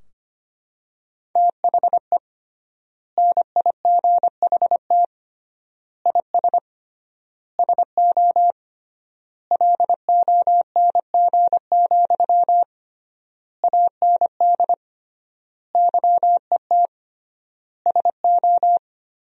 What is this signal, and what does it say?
Morse code, heard as a keyed tone, about 25 wpm, 700 Hz. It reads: THE NIGHT IS SO LONG, AND YET SO
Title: 3x Word-Spacing - Sentences from Top 500 Words - 25wpm